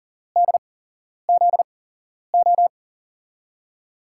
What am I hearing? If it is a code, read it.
Morse code, heard as a keyed tone, about 40 wpm, spaced as for 11 wpm, 700 Hz. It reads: DZO
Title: USA Call Signs - 6 Characters - 8x Character Spacing - 40wpm